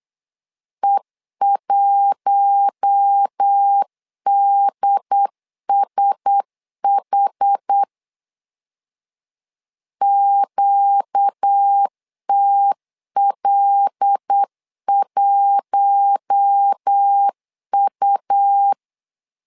Morse code, heard as a keyed tone, about 8 wpm, 800 Hz. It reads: E1DSH QTL1U
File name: LEZIONI / 5 LBVC / 5 L 8wpm 2831